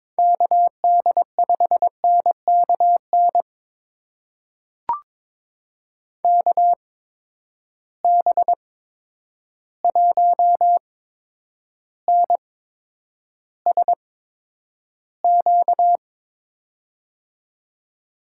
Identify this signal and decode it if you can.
Morse code, heard as a keyed tone, about 22 wpm, spaced as for 9 wpm, 700 Hz. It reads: KD5NKN KB1NSQ
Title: USA Call Signs - 6 Characters - 8x Character Spacing - 22wpm